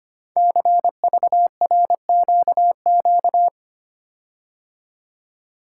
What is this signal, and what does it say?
Morse code, heard as a keyed tone, about 25 wpm, 700 Hz. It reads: CVRQQ
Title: Code Groups - 5 Letters - 25wpm